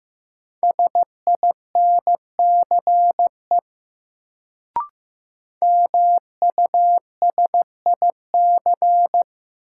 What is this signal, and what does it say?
Morse code, heard as a keyed tone, about 15 wpm, 700 Hz. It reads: SINCE MUSIC
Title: Sentences from Top 300 Words - New Word Drill - 15wpm